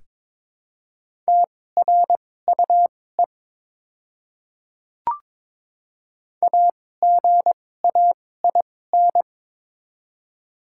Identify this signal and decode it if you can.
Morse code, heard as a keyed tone, about 23 wpm, spaced as for 16 wpm, 700 Hz. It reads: TRUE AGAIN
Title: 2x Character Spacing - Top 500 Words - 22wpm